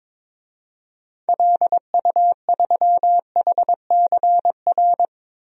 Morse code, heard as a keyed tone, about 22 wpm, 700 Hz. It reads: LU3HCR
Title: DXCC-100 Call Signs - 6 Characters - 22wpm